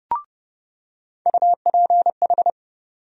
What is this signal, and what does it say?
Morse code, heard as a keyed tone, about 30 wpm, 700 Hz. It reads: UPH
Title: USA Call Sign Postfixes - 3 Letters 30wpm